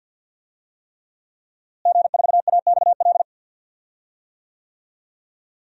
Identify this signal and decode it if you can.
Morse code, heard as a keyed tone, about 52 wpm, 700 Hz. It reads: G4AXL
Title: DXCC-100 Call Signs - 5 Characters - Repeated 3x - 50wpm